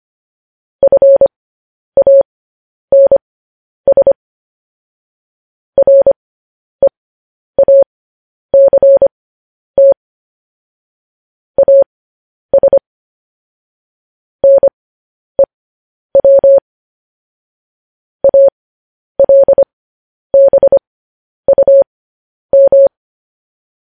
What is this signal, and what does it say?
Morse code, heard as a keyed tone, about 25 wpm, spaced as for 10 wpm, 550 Hz. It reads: FANS REACT AS NEW ALBUM